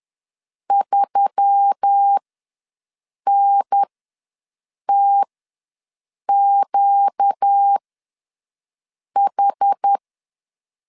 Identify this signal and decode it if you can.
Morse code, heard as a keyed tone, about 10 wpm, spaced as for 6 wpm, 800 Hz. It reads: 3NTQH
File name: LEZIONI / 5 LBVC / 5 L ws 10wpm 3636